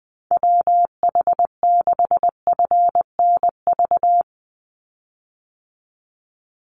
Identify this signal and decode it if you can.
Morse code, heard as a keyed tone, about 20 wpm, 700 Hz. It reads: WH6FNV
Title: USA Call Signs 20wpm